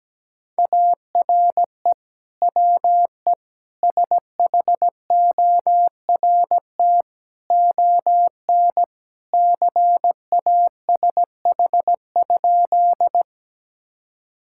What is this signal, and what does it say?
Morse code, heard as a keyed tone, about 17 wpm, 700 Hz. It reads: ARE WE SHORT ON CASH?
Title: Sentences from Top 1000 Words - Review Part 5 - 17wpm